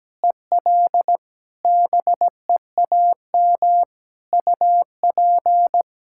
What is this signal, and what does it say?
Morse code, heard as a keyed tone, about 17 wpm, 700 Hz. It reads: EL BEAM UP
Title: Practice QSO - Exchange 2 - Info - 17wpm